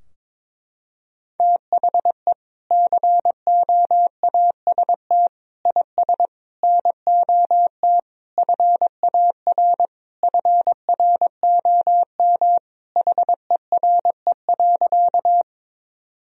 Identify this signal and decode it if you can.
Morse code, heard as a keyed tone, about 22 wpm, 700 Hz. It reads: THE COAST IS NOT FAR FROM HERE.
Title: Sentences from Top 1000 Words - Review Part 4 - 22wpm